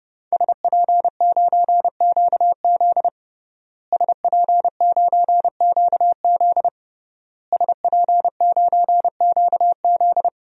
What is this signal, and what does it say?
Morse code, heard as a keyed tone, about 30 wpm, 700 Hz. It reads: SP9QZ SP9QZ SP9QZ
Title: DXCC-100 Call Signs - 5 Characters - Repeated 3x - 30wpm